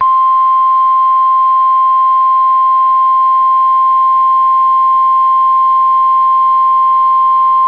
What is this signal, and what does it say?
Emergency Alert System (SAME) tones (the 1050 Hz attention tone). It is heard from 0.3 to 7.3 s.